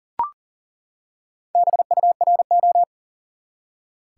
Morse code, heard as a keyed tone, about 37 wpm, 700 Hz. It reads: BURO
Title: Top 180 QSO Elements 40wpm